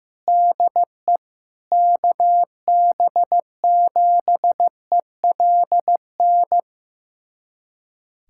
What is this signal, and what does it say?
Morse code, heard as a keyed tone, about 15 wpm, 700 Hz. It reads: DE KB7ELN
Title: USA Callsign DE Callsign 15wpm